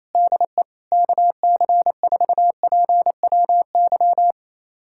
Morse code, heard as a keyed tone, about 28 wpm, 700 Hz. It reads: DE KC4PWY
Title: USA Callsign DE Callsign 28wpm